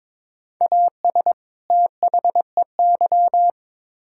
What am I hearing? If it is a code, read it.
Morse code, heard as a keyed tone, about 22 wpm, 700 Hz. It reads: AS THEY
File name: Build A Phrase - 2-Word Phrases - 22wpm